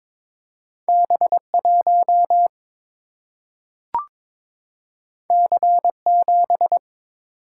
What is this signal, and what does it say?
Morse code, heard as a keyed tone, about 22 wpm, 700 Hz. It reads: B1 C7
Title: Single-Letter Single-Number 22wpm